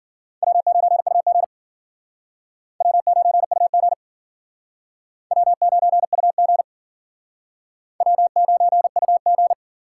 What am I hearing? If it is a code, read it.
Morse code, heard as a keyed tone, about 52 wpm, 700 Hz. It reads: W9UG W9UG W9UG W9UG